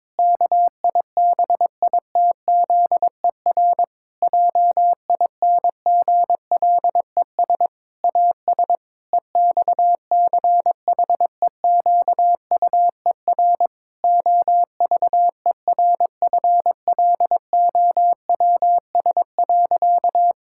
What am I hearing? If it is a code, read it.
Morse code, heard as a keyed tone, about 22 wpm, 700 Hz. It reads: KIBITZER JINGLES AS EXCHEQUER OVERFLOWS.
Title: Pangram Challenge v2 - 22wpm